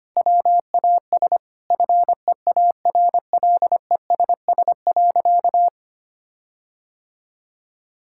Morse code, heard as a keyed tone, about 25 wpm, 700 Hz. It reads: WAS FEARLESS.
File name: practice / 25wpm / Sentences with 4-Word Phrases - 25wpm